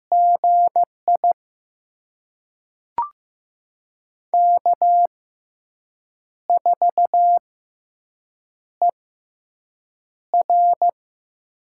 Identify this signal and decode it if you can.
Morse code, heard as a keyed tone, about 15 wpm, spaced as for 6 wpm, 700 Hz. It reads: GI K4ER
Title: USA Call Signs - 4 Characters - 6x Character Spacing - 15wpm